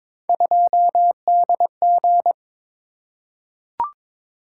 Morse code, heard as a keyed tone, about 22 wpm, 700 Hz. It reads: 2DG
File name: Three Letter-Number Combinations - 22wpm